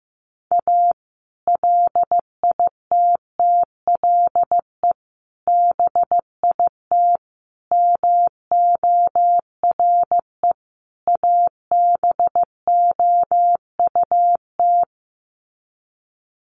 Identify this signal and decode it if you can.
Morse code, heard as a keyed tone, about 15 wpm, 700 Hz. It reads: A LITTLE BIT MORE ABOUT
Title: Sets of 5 Words Encore2 15wpm